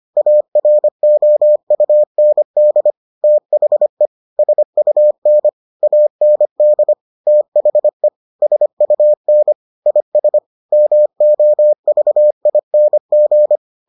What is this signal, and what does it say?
Morse code, heard as a keyed tone, about 25 wpm, 600 Hz. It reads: AROUND THE SUN AND THE SUN IS MOVING